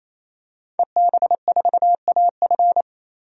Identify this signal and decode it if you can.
Morse code, heard as a keyed tone, about 28 wpm, 700 Hz. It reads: EB4AF
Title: DXCC-100 Call Signs - 5 Characters - 28wpm